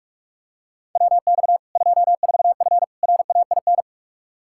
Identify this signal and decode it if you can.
Morse code, heard as a keyed tone, about 46 wpm, 700 Hz. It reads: WX 24F RAIN
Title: Practice QSO - Exchange 2 - Info - 45wpm